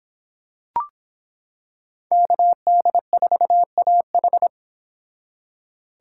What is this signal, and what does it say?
Morse code, heard as a keyed tone, about 25 wpm, 700 Hz. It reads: KD4AH